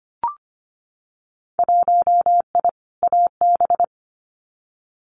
Morse code, heard as a keyed tone, about 25 wpm, 700 Hz. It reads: 1I AB
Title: Winter Field Day Exchanges Encore 25wpm